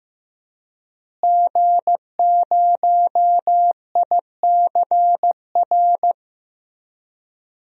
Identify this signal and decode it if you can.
Morse code, heard as a keyed tone, about 15 wpm, 700 Hz. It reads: G0ICR
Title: DXCC-100 Call Signs - 5 Characters - Repeated 3x - 15wpm